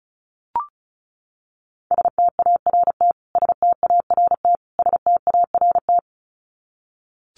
Morse code, heard as a keyed tone, about 34 wpm, 700 Hz. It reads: START START START
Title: Top 500 Words Repeated 35wpm